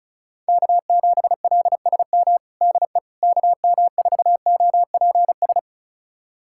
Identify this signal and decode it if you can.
Morse code, heard as a keyed tone, about 34 wpm, 700 Hz. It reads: K7LSM DE KM4OPS
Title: USA Callsign DE Callsign 35wpm